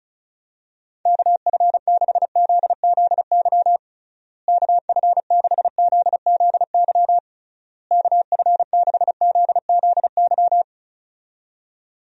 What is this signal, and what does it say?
Morse code, heard as a keyed tone, about 36 wpm, 700 Hz. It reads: KF6ZZY KF6ZZY KF6ZZY
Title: USA Call Signs - 6 Characters - Repeated 3x - 35wpm